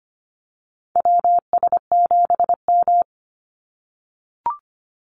Morse code, heard as a keyed tone, about 25 wpm, 700 Hz. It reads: WS7M
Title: Call Signs - 25wpm